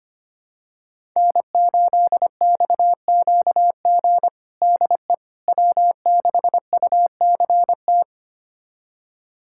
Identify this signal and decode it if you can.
Morse code, heard as a keyed tone, about 25 wpm, 700 Hz. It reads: N8XQG DE W6UCT
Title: USA Callsign DE Callsign - 5 Characters - 25wpm